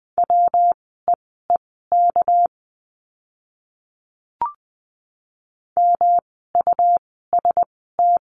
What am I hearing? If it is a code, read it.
Morse code, heard as a keyed tone, about 20 wpm, spaced as for 15 wpm, 700 Hz. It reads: WEEK MUST
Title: 237 Lesson 38f - Letter q - Words 20wpm